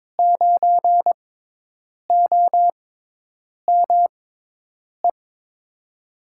Morse code, heard as a keyed tone, about 21 wpm, spaced as for 8 wpm, 700 Hz. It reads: 9OME
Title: USA Call Signs - 6 Characters - 6x Character Spacing - 22wpm